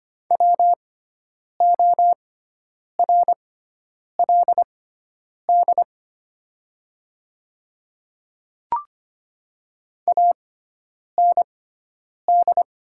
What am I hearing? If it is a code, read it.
Morse code, heard as a keyed tone, about 25 wpm, spaced as for 9 wpm, 700 Hz. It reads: WORLD AND
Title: 6x Character Spacing - Top 200 Words - 25wpm